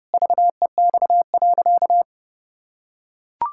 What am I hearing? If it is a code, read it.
Morse code, heard as a keyed tone, about 30 wpm, 700 Hz. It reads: VEX.
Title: Pangram Challenge v2 - 30wpm